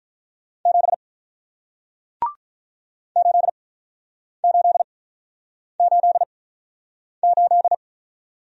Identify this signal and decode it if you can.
Morse code, heard as a keyed tone, about 48 wpm, 700 Hz. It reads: 7 8 8 8 8